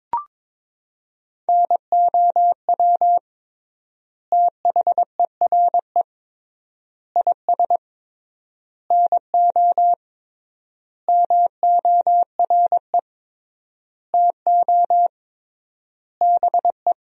Morse code, heard as a keyed tone, about 21 wpm, 700 Hz. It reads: NOW THERE IS NO MORE TO BE
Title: 3x Word-Spacing - Sentences from Top 100 Words - 22wpm